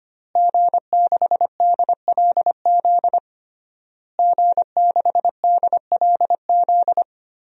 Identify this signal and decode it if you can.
Morse code, heard as a keyed tone, about 25 wpm, 700 Hz. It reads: G6DLZ G6DLZ
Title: DXCC-100 Call Signs - 5 Characters - Repeated 3x - 25wpm